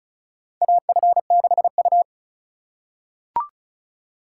Morse code, heard as a keyed tone, about 36 wpm, 700 Hz. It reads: AF6U
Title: USA Call Signs - 4 Characters - 4x Character Spacing - 35wpm